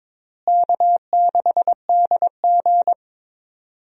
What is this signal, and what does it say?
Morse code, heard as a keyed tone, about 22 wpm, 700 Hz. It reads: K6DG
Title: USA Call Signs - 4 Characters - Repeated 3x - 22wpm